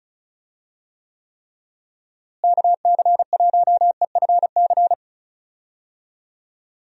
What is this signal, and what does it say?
Morse code, heard as a keyed tone, about 34 wpm, 700 Hz. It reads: KC1EFC
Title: USA Call Signs - 6 Characters - Repeated 3x - 35wpm